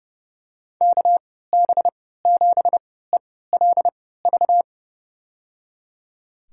Morse code, heard as a keyed tone, about 30 wpm, spaced as for 17 wpm, 700 Hz. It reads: KB7ELV
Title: USA Call Signs - 6 Characters - 3x Character Spacing - 30wpm